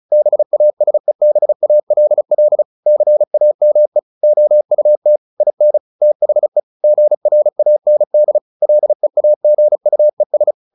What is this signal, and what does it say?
Morse code, heard as a keyed tone, about 34 wpm, 600 Hz. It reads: BASEBALL CAME OUT IN THE GRAND LEAGUES
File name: the old man and the sea / the old man and the sea 1 35wpm